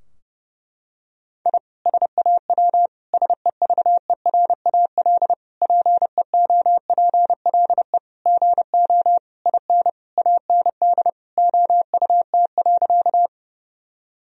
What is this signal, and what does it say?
Morse code, heard as a keyed tone, about 30 wpm, 700 Hz. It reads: I SAW SEVERAL PEOPLE GO IN AND OUT.
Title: Sentences from Top 1000 Words - Review Part 2 - 30wpm